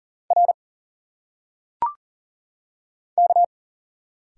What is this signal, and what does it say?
Morse code, heard as a keyed tone, about 40 wpm, 700 Hz. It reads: R K
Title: Single Letters - Phonetic Pronunciation 40wpm